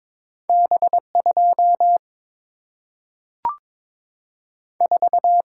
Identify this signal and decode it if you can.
Morse code, heard as a keyed tone, about 22 wpm, 700 Hz. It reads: B 2 4
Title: Single-Letter Single-Number 22wpm